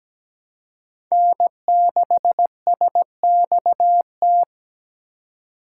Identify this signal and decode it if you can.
Morse code, heard as a keyed tone, about 20 wpm, 700 Hz. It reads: N6SXT
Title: Speed Racing - USA Call Signs 17wpm